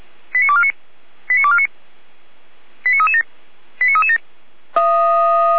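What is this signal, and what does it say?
Five-tone selective call (ZVEI), heard as quick runs of five tones, tones of about 70 ms. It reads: 89249 89249 89337 89337
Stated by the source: Received by SDR as ZVEI five-tone selective call 89501.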